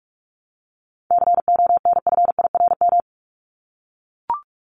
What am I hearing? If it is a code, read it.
Morse code, heard as a keyed tone, about 44 wpm, 700 Hz. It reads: CONFIRM
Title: Top 1751-2000 Words 45wpm